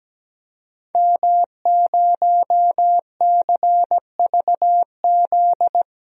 Morse code, heard as a keyed tone, about 17 wpm, 700 Hz. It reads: M0CVZ
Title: DXCC-100 Call Signs - 5 Characters - 17wpm